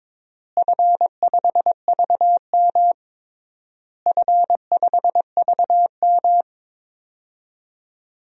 Morse code, heard as a keyed tone, about 22 wpm, 700 Hz. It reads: F5VM F5VM
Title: DXCC-100 Call Signs - 4 Characters - Repeated 3x - 22wpm